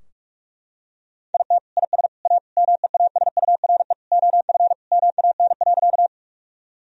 Morse code, heard as a keyed tone, about 41 wpm, 700 Hz. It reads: IT IS A MEASURE OF MAN.
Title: Sentences from Top 1000 Words - Review Part 2 - 45wpm